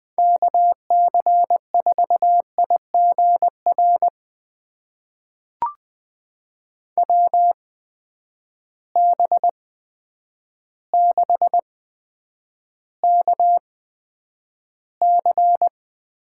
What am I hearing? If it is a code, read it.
Morse code, heard as a keyed tone, about 20 wpm, spaced as for 9 wpm, 700 Hz. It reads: KC4IGR WB6KC
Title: USA Call Signs - 6 Characters - 8x Character Spacing - 20wpm